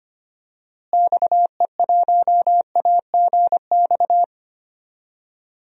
Morse code, heard as a keyed tone, about 25 wpm, 700 Hz. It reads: XE1AGX
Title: DXCC-100 Call Signs - 6 Characters - Repeated 2x - 25wpm